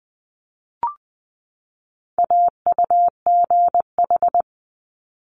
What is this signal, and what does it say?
Morse code, heard as a keyed tone, about 20 wpm, 700 Hz. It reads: AUGH
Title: Multi-letter Phonemes 20wpm